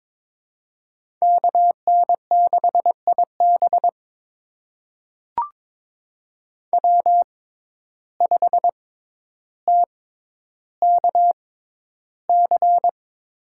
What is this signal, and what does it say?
Morse code, heard as a keyed tone, about 22 wpm, spaced as for 11 wpm, 700 Hz. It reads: KN6IB W5TKC
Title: USA Call Signs - 5 Characters - 6x Character Spacing - 22wpm